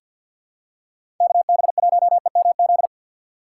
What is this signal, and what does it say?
Morse code, heard as a keyed tone, about 50 wpm, 700 Hz. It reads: KB1EMZ
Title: USA Call Signs - 6 Characters - 3x Character Spacing - 50wpm